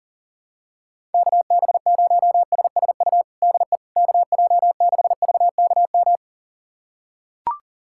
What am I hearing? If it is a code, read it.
Morse code, heard as a keyed tone, about 40 wpm, 700 Hz. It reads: KB0SSU DE KJ6VKM
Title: USA Callsign DE Callsign - 6 Characters - 40wpm